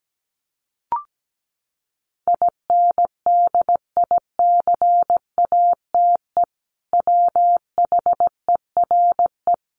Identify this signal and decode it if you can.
Morse code, heard as a keyed tone, about 17 wpm, 700 Hz. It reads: INDICATE WHERE
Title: Sentences from Top 1300 Words 17wpm